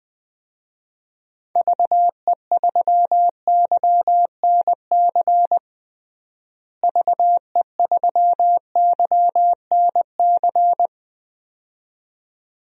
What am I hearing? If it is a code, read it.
Morse code, heard as a keyed tone, about 21 wpm, 690 Hz. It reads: VE3YNC VE3YNC
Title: DXCC-100 Call Signs - 6 Characters - Repeated 2x - 20wpm